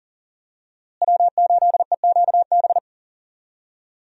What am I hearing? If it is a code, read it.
Morse code, heard as a keyed tone, about 37 wpm, 700 Hz. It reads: W8EQB
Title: USA Call Signs - 5 Characters - Repeated 2x - 40wpm